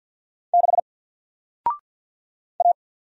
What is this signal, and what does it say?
Morse code, heard as a keyed tone, about 50 wpm, 700 Hz. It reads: 6 A
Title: Single Letter-Number Alt 50wpm